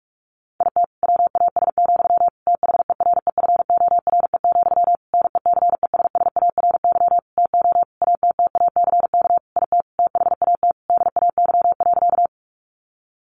Morse code, heard as a keyed tone, about 44 wpm, 700 Hz. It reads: IT WAS, THEREFORE, NECESSARY TO ATTACK IT THAT DAY.
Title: Sentences from Top 1500 Words 45wpm